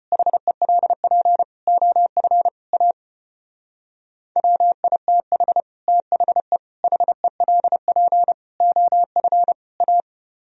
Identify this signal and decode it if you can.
Morse code, heard as a keyed tone, about 31 wpm, 700 Hz. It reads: HELP OF A WITH THE HELP OF A
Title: Speed-Racing - Sets of 5 Words 25wpm